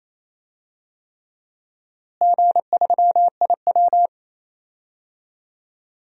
Morse code, heard as a keyed tone, about 28 wpm, 700 Hz. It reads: G3IW
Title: DXCC-100 Call Signs - 4 Characters - Repeated 3x - 28wpm